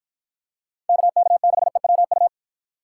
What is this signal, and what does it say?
Morse code, heard as a keyed tone, about 53 wpm, 700 Hz. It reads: KK6EWU